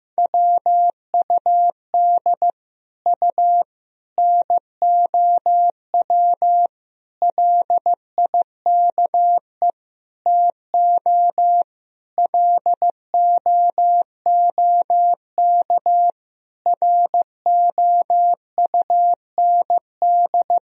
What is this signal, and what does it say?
Morse code, heard as a keyed tone, about 15 wpm, 700 Hz. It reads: WUD U NOW LIKE TO LOOK ROUND
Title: Sentences from Top 200 Words and Abbreviations 15wpm